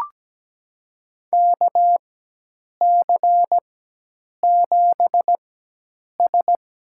Morse code, heard as a keyed tone, about 17 wpm, spaced as for 8 wpm, 700 Hz. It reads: KC7S
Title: USA Call Signs - 4x Character Spacing - 17wpm